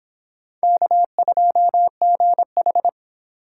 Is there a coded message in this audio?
Yes: Morse code from 0.6 to 2.9 s, heard as a keyed tone, about 26 wpm, 700 Hz. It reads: K2GH